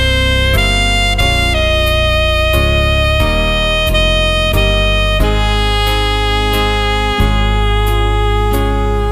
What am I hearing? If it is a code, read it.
Dial-up modem handshake.